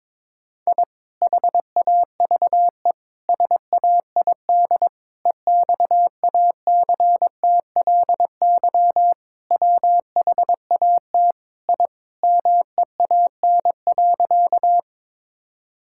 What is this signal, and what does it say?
Morse code, heard as a keyed tone, about 22 wpm, 700 Hz. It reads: I HAVE SAID EXACTLY WHAT I MEAN.
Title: Sentences from Top 1000 Words - Review Part 6 - 22wpm